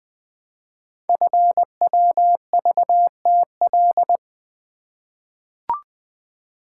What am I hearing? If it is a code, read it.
Morse code, heard as a keyed tone, about 20 wpm, 700 Hz. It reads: FWVTL
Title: Code Groups - 5 Letter:Numbers - 20wpm